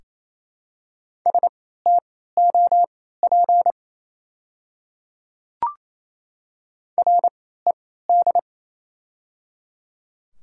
Morse code, heard as a keyed tone, about 28 wpm, spaced as for 16 wpm, 700 Hz. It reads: STOP RED
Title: 3x Character Spacing - Top 500 Words - 28wpm